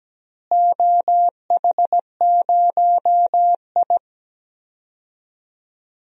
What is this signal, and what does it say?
Morse code, heard as a keyed tone, about 17 wpm, 700 Hz. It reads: OH0I
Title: DXCC-100 Call Signs - 4 Characters - Repeated 3x - 17wpm